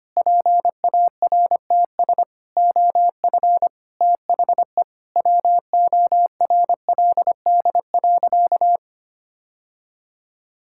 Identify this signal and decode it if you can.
Morse code, heard as a keyed tone, about 25 wpm, 700 Hz. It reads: PARTS OF THE WORLD.
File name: practice / 25wpm / Sentences with 3-Word Phrases - 25wpm